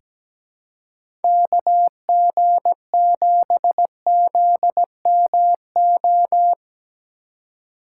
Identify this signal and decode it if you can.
Morse code, heard as a keyed tone, about 17 wpm, 700 Hz. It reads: KG7ZMO
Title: USA Call Signs - 6 Characters - 2x Character Spacing - 17wpm